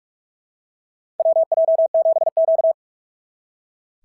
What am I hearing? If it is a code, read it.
Morse code, heard as a keyed tone, about 44 wpm, 690 Hz. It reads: WJ7Q
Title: SOTA QSO Practice 45wpm